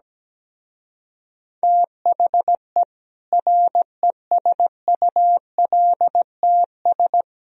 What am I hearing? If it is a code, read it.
Morse code, heard as a keyed tone, about 17 wpm, 700 Hz. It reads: THE RESULTS
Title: Speed-Racing - Sets of 4 Words 17wpm